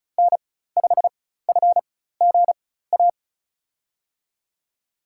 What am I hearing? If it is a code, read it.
Morse code, heard as a keyed tone, about 36 wpm, spaced as for 16 wpm, 700 Hz. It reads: N5FGA
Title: USA Call Signs - 4x Character Spacing - 35wpm